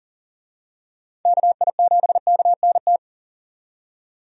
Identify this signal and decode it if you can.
Morse code, heard as a keyed tone, about 39 wpm, 700 Hz. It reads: KI7KNT
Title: USA Call Signs - 6 Characters - 40wpm